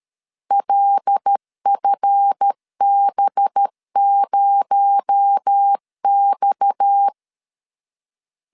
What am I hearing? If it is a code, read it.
Morse code, heard as a keyed tone, about 13 wpm, 800 Hz. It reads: LFB0X
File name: LEZIONI / 8 YJ45 / 8 YJ4 12wpm 2151